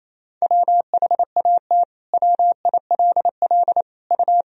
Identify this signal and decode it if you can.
Morse code, heard as a keyed tone, about 28 wpm, 700 Hz. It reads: WHAT WILL U
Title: Sentences from Top 100 Words and Abbreviations 28wpm